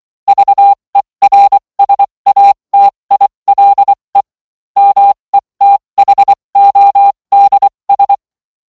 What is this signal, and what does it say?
Morse code, heard as a keyed tone, about 24 wpm, spaced as for 20 wpm, 800 Hz. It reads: VERSATILE METHODS